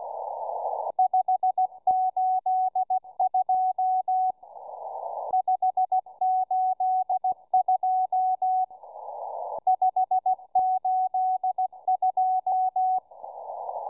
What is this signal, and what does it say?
Morse code, heard as a keyed tone, about 17 wpm, 750 Hz, with radio noise. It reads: 582 582 582